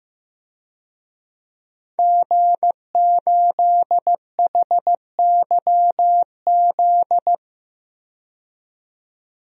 Morse code, heard as a keyed tone, about 15 wpm, 700 Hz. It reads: G8HYZ
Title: DXCC-100 Call Signs - 5 Characters - 15wpm